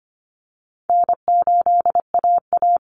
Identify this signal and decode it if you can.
Morse code, heard as a keyed tone, about 25 wpm, 700 Hz. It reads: N8AA
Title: Easy Call Signs Repeated 3x - 25wpm